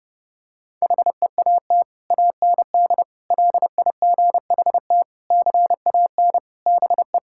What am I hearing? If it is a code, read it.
Morse code, heard as a keyed tone, about 30 wpm, 700 Hz. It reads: HEAT AND LIGHT CAN BE
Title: Sentences from Top 1000 Words - Review Part 6 - 30wpm